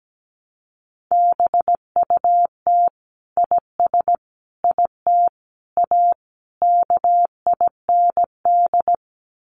Morse code, heard as a keyed tone, about 17 wpm, 700 Hz. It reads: BUT IS IT A KIND
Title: Sentences from Top 1200 Words 17wpm